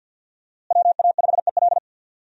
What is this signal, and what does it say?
Morse code, heard as a keyed tone, about 53 wpm, 700 Hz. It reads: WA5EL
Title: USA Call Signs - 5 Characters - Repeated 3x - 50wpm